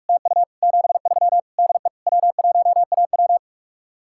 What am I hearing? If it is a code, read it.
Morse code, heard as a keyed tone, about 44 wpm, 700 Hz. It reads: TU 73 DE W1AW